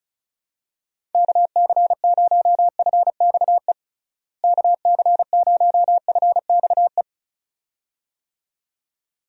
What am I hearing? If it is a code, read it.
Morse code, heard as a keyed tone, about 36 wpm, 700 Hz. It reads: KC0FXE KC0FXE
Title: USA Call Signs - 6 Characters - Repeated 2x - 35wpm